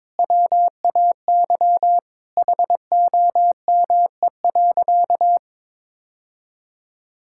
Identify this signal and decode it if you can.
Morse code, heard as a keyed tone, about 22 wpm, 700 Hz. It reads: WAY HOME.
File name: practice / 22wpm / Sentences with 3-Word Phrases - 22wpm